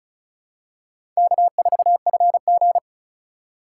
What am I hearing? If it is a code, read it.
Morse code, heard as a keyed tone, about 36 wpm, 690 Hz. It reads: K4FG
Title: USA Call Signs - 4 Characters - 2x Character Spacing - 35wpm